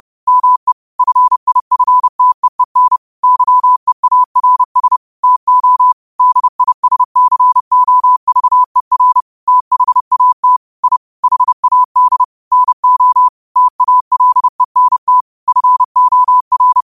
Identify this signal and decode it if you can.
Morse code, heard as a keyed tone, about 30 wpm, 1000 Hz. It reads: ME FIFTEEN YEARS TO DISCOVER THAT I HAD NO TALENT FOR
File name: qotdfiles / 30wpm